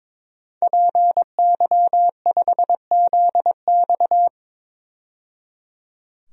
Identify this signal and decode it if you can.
Morse code, heard as a keyed tone, about 22 wpm, 700 Hz. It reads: PY5ZX
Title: DXCC-100 Call Signs - 5 Characters - 22wpm